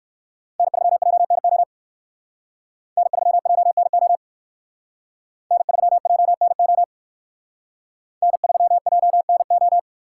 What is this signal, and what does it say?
Morse code, heard as a keyed tone, about 55 wpm, 700 Hz. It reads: N3JNO N3JNO N3JNO N3JNO
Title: Speed Racing - USA Call Signs 45wpm